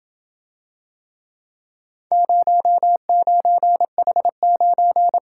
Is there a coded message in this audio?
Yes: Morse code from 2.1 to 5.2 s, heard as a keyed tone, about 28 wpm, 700 Hz. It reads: 09H9